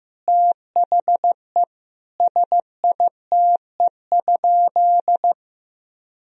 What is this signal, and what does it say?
Morse code, heard as a keyed tone, about 15 wpm, 700 Hz. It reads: THE SITE?
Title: Sentences from Top 1000 Words - Review Part 4 - 15wpm